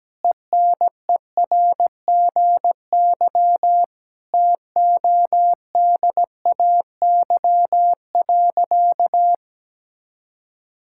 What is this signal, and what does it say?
Morse code, heard as a keyed tone, about 17 wpm, 700 Hz. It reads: ENERGY TODAY.
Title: Sentences from Top 1000 Words - Review Part 6 - 17wpm